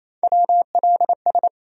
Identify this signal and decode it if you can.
Morse code, heard as a keyed tone, about 27 wpm, 700 Hz. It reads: WLS
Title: USA Call Sign Postfixes - 3 Letters 28wpm